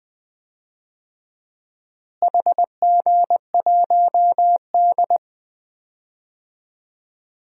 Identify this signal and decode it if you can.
Morse code, heard as a keyed tone, about 20 wpm, 700 Hz. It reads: HG1D
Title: DXCC-100 Call Signs - 4 Characters - Repeated 2x - 20wpm